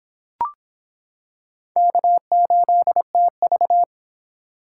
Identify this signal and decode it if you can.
Morse code, heard as a keyed tone, about 24 wpm, 700 Hz. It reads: K8TV